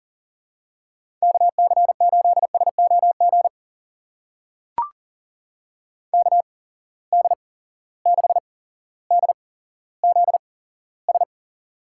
Morse code, heard as a keyed tone, about 40 wpm, spaced as for 17 wpm, 700 Hz. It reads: KC8SOG KD6DZS
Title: USA Call Signs - 6 Characters - 8x Character Spacing - 40wpm